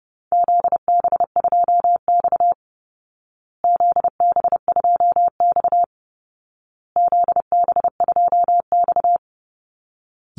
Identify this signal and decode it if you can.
Morse code, heard as a keyed tone, about 30 wpm, 700 Hz. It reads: ZB2X ZB2X ZB2X
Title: Call Signs Repeated 2x - 30wpm